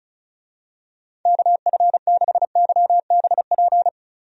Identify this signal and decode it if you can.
Morse code, heard as a keyed tone, about 36 wpm, 700 Hz. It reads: KF6YBP
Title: USA Call Signs - 6 Characters - 35wpm